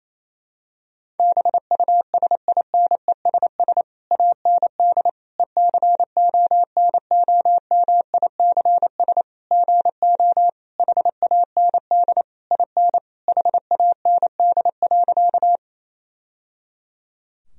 Morse code, heard as a keyed tone, about 28 wpm, 700 Hz. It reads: BUSINESS AND ECONOMICS GO HAND IN HAND.
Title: Sentences from Top 1000 Words - Review Part 3 - 28wpm